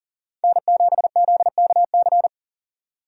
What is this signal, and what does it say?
Morse code, heard as a keyed tone, about 38 wpm, 700 Hz. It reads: N7ZKC
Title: USA Call Signs - 5 Characters - 2x Character Spacing - 40wpm